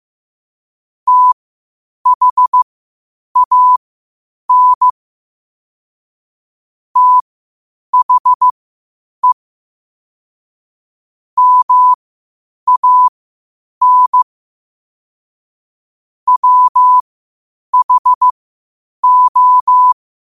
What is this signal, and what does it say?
Morse code, heard as a keyed tone, about 15 wpm, spaced as for 8 wpm, 1000 Hz. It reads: THAN THE MAN WHO
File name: qotdfiles / 8wpm